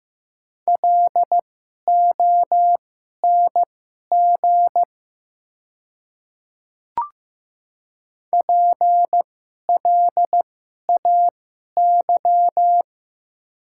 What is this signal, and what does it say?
Morse code, heard as a keyed tone, about 15 wpm, spaced as for 11 wpm, 700 Hz. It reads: LONG PLAY
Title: 2x Character Spacing - Top 200 Words - 15wpm